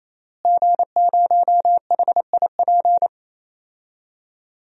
Morse code, heard as a keyed tone, about 28 wpm, 700 Hz. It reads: G0HIP
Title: DXCC-100 Call Signs - 5 Characters - Repeated 3x - 28wpm